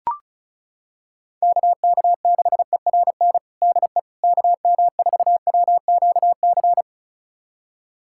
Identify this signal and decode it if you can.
Morse code, heard as a keyed tone, about 34 wpm, 700 Hz. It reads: KK6ERN DE KM4WQC
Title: USA Callsign DE Callsign - 6 Characters - 35wpm